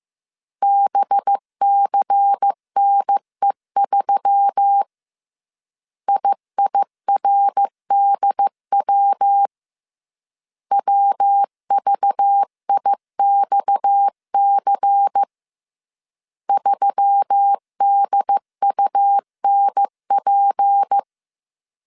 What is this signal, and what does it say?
Morse code, heard as a keyed tone, about 15 wpm, 800 Hz. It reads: BCNE3 IIRDW WVIXC 3DUNP